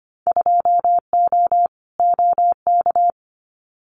Morse code, heard as a keyed tone, about 25 wpm, 700 Hz. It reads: 2O OK
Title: Winter Field Day Exchanges 25wpm